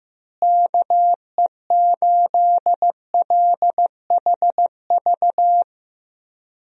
Morse code, heard as a keyed tone, about 15 wpm, 700 Hz. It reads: KE8LHV
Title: USA Call Signs - 6 Characters - Repeated 3x - 15wpm